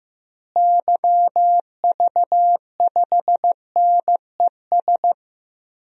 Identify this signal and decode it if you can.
Morse code, heard as a keyed tone, about 15 wpm, 700 Hz. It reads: YV5NES
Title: DXCC-100 Call Signs - 6 Characters - Repeated 2x - 15wpm